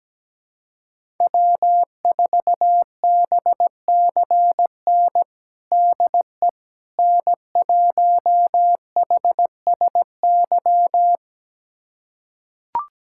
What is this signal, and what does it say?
Morse code, heard as a keyed tone, about 17 wpm, 700 Hz. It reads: W4BCN DE N1HSY
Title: USA Callsign DE Callsign - 5 Characters - 17wpm